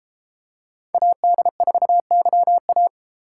Morse code, heard as a keyed tone, about 31 wpm, 700 Hz. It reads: AD4YA